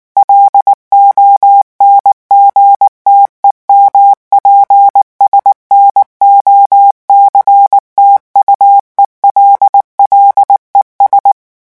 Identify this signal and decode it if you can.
Morse code, heard as a keyed tone, about 19 wpm, 800 Hz. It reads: LONGTEMPS NOCTUELLES
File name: morse mp3 / revision9 citations humoristiques vitesse 18 mots minute